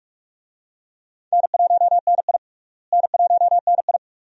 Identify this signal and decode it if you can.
Morse code, heard as a keyed tone, about 44 wpm, 700 Hz. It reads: N1NI N1NI
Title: USA Call Signs - 4 Characters - Repeated 2x - 45wpm